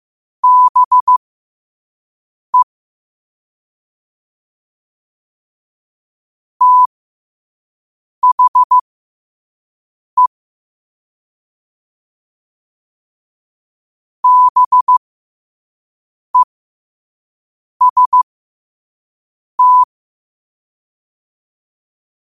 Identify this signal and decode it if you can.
Morse code, heard as a keyed tone, about 15 wpm, spaced as for 5 wpm, 1000 Hz. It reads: BE THE BEST
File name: qotdfiles / 5wpm